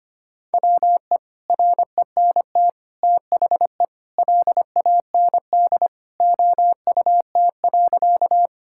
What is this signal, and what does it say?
Morse code, heard as a keyed tone, about 25 wpm, 700 Hz. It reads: WE RENT THE LAND OUT.
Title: Sentences from Top 1000 Words - Review Part 5 - 25wpm